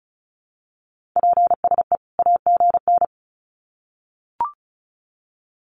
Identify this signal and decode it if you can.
Morse code, heard as a keyed tone, about 34 wpm, 700 Hz. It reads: PSE AGN
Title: Common QSO Phrases 35wpm